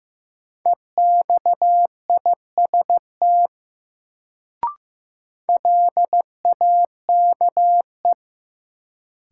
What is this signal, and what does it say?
Morse code, heard as a keyed tone, about 15 wpm, 700 Hz. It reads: EXIST LAKE
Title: Sentences from Top 1100 Words - New Word Drill Alt - 15wpm